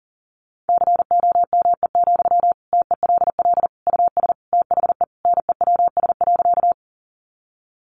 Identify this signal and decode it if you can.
Morse code, heard as a keyed tone, about 40 wpm, 700 Hz. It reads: COME, TELL US THE NEWS.
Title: Sentences from Top 800 Words 40wpm